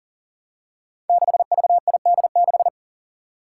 Morse code, heard as a keyed tone, about 41 wpm, 700 Hz. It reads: 6VID6
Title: Five Challenging Letter-Number Combinations 40wpm